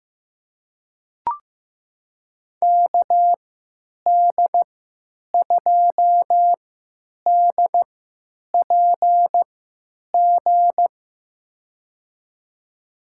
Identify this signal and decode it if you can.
Morse code, heard as a keyed tone, about 15 wpm, spaced as for 9 wpm, 700 Hz. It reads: KD2DPG
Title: USA Call Signs - 6 Characters - 3x Character Spacing - 15wpm